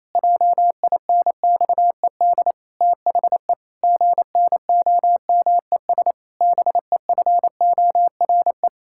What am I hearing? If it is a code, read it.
Morse code, heard as a keyed tone, about 28 wpm, 700 Hz. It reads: JINXED THE GNOMES BEFORE
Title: Pangram Challenge v2 - 28wpm